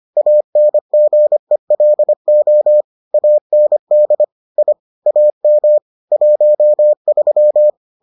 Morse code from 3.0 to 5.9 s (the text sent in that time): AND I AM